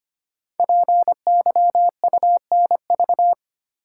Morse code, heard as a keyed tone, about 25 wpm, 700 Hz. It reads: PYUNV
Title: Code Groups - 5 Letters - 25wpm